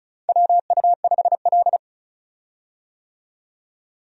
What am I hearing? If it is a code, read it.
Morse code, heard as a keyed tone, about 34 wpm, 700 Hz. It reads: WU5L
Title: USA Call Signs - 4 Characters - 35wpm